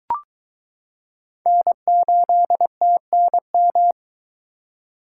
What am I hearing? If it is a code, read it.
Morse code, heard as a keyed tone, about 22 wpm, 700 Hz. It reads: N8TNM